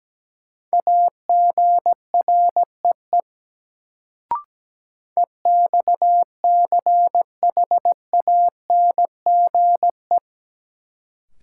Morse code, heard as a keyed tone, about 17 wpm, 700 Hz. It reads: AGREE EXCHANGE
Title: Sentences from Top 1000 Words - Review Part 6 - New Word Drill - 17wpm